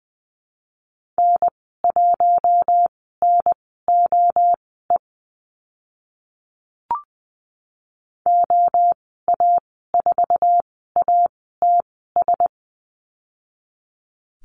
Morse code, heard as a keyed tone, about 20 wpm, spaced as for 15 wpm, 700 Hz. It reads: N1NOE OA4ATS
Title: 055 Lesson 10g - Letter r - Call Signs 20wpm